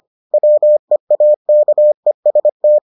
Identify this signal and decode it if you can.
Morse code, heard as a keyed tone, about 25 wpm, 600 Hz. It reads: WEAKEST